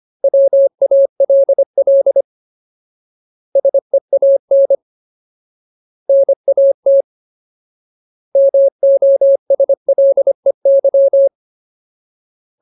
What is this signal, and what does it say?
Morse code, heard as a keyed tone, about 25 wpm, 550 Hz. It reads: WALL SEAN NAT MOSLEY
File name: Ultimate Word List / 25WPM Ultimate